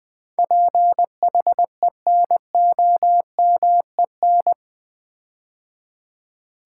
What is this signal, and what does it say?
Morse code, heard as a keyed tone, about 20 wpm, 700 Hz. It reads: PHENOMEN
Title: Build A Word - 4 Letters At A Time - 10-Letter Words - 20wpm